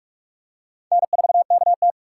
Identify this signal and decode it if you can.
Morse code, heard as a keyed tone, about 46 wpm, 700 Hz. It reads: N4KT
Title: USA Call Signs - 4 Characters - 2x Character Spacing - 45wpm